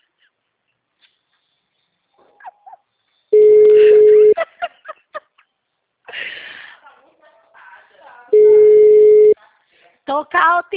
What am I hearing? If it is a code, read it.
Ringback tone (Europe, 425 Hz), with speech.
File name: out- -2002- - - .21948